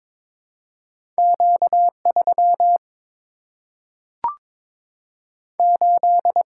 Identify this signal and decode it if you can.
Morse code, heard as a keyed tone, about 22 wpm, 700 Hz. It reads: Q 3 8
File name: practice / 22wpm / Two Letter-Number Combinations - 22wpm